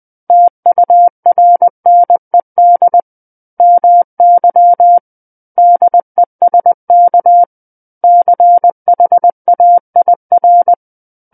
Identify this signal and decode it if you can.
Morse code, heard as a keyed tone, about 20 wpm, 700 Hz. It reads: TURNED MY DESK CHAIR
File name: 20 WPM